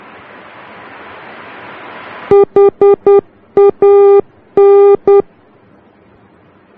Morse code, heard as a keyed tone, about 10 wpm, 400 Hz. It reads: HAN
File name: germany / mx / HAN-D